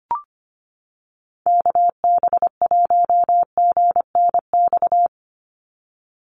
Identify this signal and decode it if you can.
Morse code, heard as a keyed tone, about 25 wpm, 700 Hz. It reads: KB1GNX
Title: USA Call Signs 25wpm